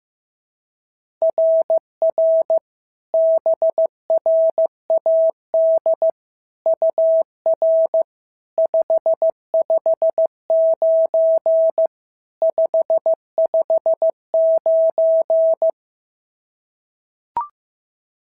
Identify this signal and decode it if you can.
Morse code, heard as a keyed tone, about 14 wpm, 650 Hz. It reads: RR BRAD UR 559 559
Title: SOTA QSO Practice 15wpm